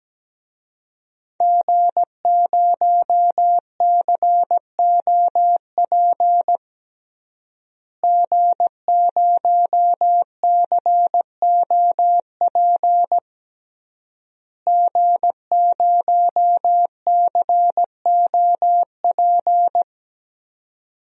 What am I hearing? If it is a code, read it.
Morse code, heard as a keyed tone, about 17 wpm, 700 Hz. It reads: G0COP G0COP G0COP
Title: DXCC-100 Call Signs - 5 Characters - Repeated 3x - 17wpm